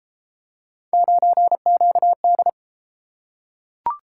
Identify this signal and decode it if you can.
Morse code, heard as a keyed tone, about 30 wpm, 700 Hz. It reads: 9QD